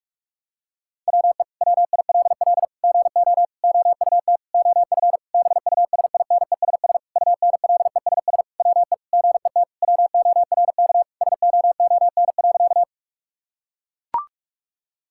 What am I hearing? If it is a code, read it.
Morse code, heard as a keyed tone, about 41 wpm, 700 Hz. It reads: WE WILL GO OUT OF BUSINESS UNLESS WE GET WORK SOON.
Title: Sentences from Top 1000 Words - Review Part 6 - 45wpm